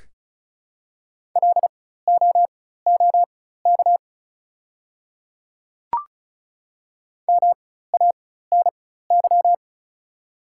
Morse code, heard as a keyed tone, about 34 wpm, spaced as for 16 wpm, 700 Hz. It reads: LOOK MANY
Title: 4x Character Spacing - Top 100 Words - 35wpm